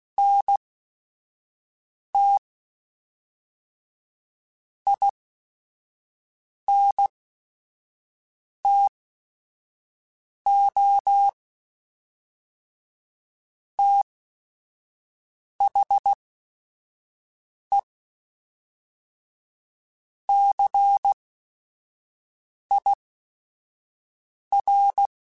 Morse code, heard as a keyed tone, about 16 wpm, spaced as for 5 wpm, 800 Hz. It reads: NT INTO THE CIR
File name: Technical Subjects / Technical Subjects-mp3s / XC @5